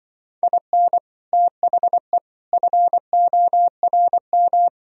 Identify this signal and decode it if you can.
Morse code, heard as a keyed tone, about 24 wpm, 700 Hz. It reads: IN THE FORM
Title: Speed-Racing - Sets of 5 Words 20wpm